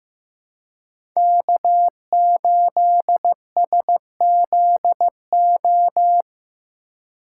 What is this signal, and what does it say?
Morse code, heard as a keyed tone, about 15 wpm, 700 Hz. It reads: K8SZO
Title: USA Call Signs - 5 Characters - 3x Character Spacing - 15wpm